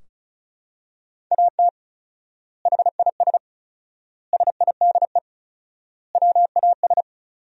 Morse code, heard as a keyed tone, about 36 wpm, 700 Hz. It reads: AT HIS SIDE WAS
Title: 4x Word-Spacing - Sentences from Top 500 Words - 35wpm